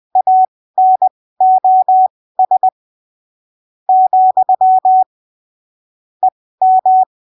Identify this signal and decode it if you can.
Morse code, heard as a keyed tone, about 20 wpm, spaced as for 15 wpm, 750 Hz. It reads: ANOS , EM
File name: arquivos / texto / texto 08 20ppm